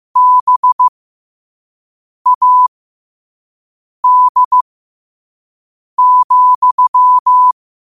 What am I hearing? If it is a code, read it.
Morse code, heard as a keyed tone, about 15 wpm, spaced as for 5 wpm, 1000 Hz. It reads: BAD,
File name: qotdfiles / 5wpm